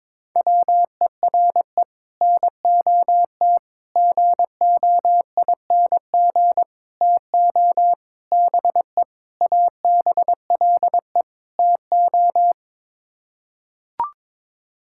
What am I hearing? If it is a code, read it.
Morse code, heard as a keyed tone, about 22 wpm, 700 Hz. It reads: WERE NOT GOING TO BE ABLE TO
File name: practice / 22wpm / Sets of 7 Words - 22wpm